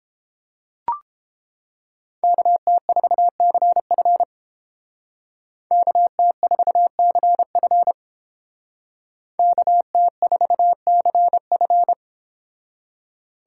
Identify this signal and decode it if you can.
Morse code, heard as a keyed tone, about 30 wpm, 700 Hz. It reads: KT4CF KT4CF KT4CF